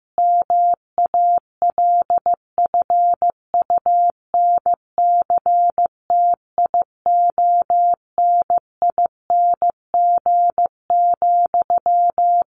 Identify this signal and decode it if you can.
Morse code, heard as a keyed tone, about 15 wpm, 700 Hz. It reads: MALFUNCTIONING,
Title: A Sound Of Thunder 15wpm